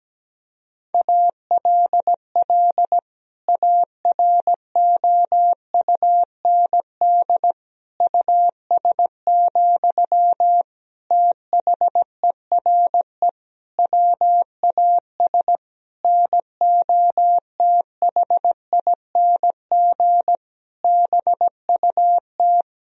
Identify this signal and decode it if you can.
Morse code, heard as a keyed tone, about 17 wpm, 700 Hz. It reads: ALL AROUND US, THERE WAS NOTHING BUT
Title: Sentences from Top 1000 Words - Review Part 3 - 17wpm